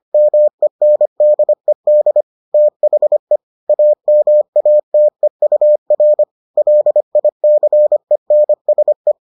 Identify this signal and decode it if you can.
Morse code, heard as a keyed tone, about 25 wpm, 600 Hz. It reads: MENDED THE AMATEUR LICENSE